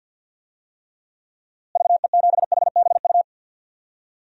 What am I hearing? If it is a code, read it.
Morse code, heard as a keyed tone, about 53 wpm, 700 Hz. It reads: VE7HBU
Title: DXCC-100 Call Signs - 6 Characters - Repeated 2x - 50wpm